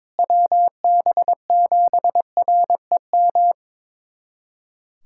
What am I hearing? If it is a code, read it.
Morse code, heard as a keyed tone, about 22 wpm, 700 Hz. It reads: WB7REM
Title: USA Callsign DE Callsign - 6 Characters - 22wpm